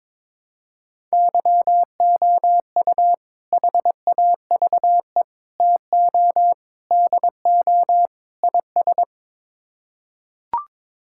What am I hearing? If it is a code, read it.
Morse code, heard as a keyed tone, about 22 wpm, 700 Hz. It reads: YOU HAVE TO DO IS
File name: practice / 22wpm / Build A Phrase - 5-Word Phrases - 22wpm